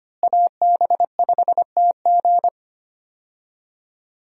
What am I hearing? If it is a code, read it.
Morse code, heard as a keyed tone, about 25 wpm, 700 Hz. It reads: AB5TG
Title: USA Call Signs - 5 Characters - Repeated 3x - 25wpm